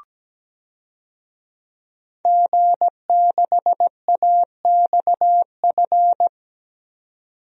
Morse code, heard as a keyed tone, about 17 wpm, 700 Hz. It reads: G6AXF
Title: DXCC-100 Call Signs - 5 Characters - 17wpm